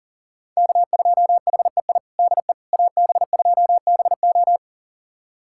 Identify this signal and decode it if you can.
Morse code, heard as a keyed tone, about 40 wpm, 700 Hz. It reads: K2HEI DE AB2BO
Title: USA Callsign DE Callsign - 5 Characters - 40wpm